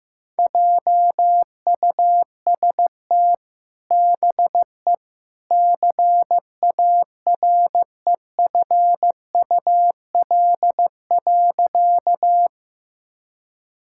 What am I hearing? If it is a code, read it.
Morse code, heard as a keyed tone, about 15 wpm, 700 Hz. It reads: JUST BE CAREFUL.
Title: Sentences from Top 1600 Words 15wpm